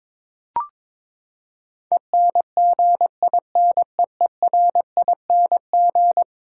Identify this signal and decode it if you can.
Morse code, heard as a keyed tone, about 22 wpm, 700 Hz. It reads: ENGINEERING
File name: practice / 22wpm / Eleven Letter Words - 22wpm